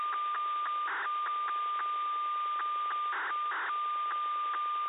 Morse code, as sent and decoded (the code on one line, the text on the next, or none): none